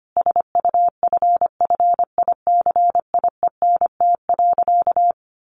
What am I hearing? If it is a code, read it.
Morse code, heard as a keyed tone, about 25 wpm, 700 Hz. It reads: SUFFICIENT.
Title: Sentences from Top 1400 Words 25wpm